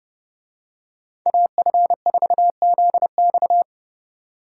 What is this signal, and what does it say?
Morse code, heard as a keyed tone, about 29 wpm, 700 Hz. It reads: AF4ZX